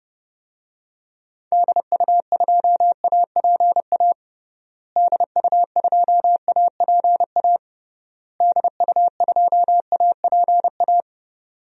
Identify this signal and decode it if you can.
Morse code, heard as a keyed tone, about 30 wpm, 700 Hz. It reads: DU2APA DU2APA DU2APA
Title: DXCC-100 Call Signs - 6 Characters - Repeated 3x - 30wpm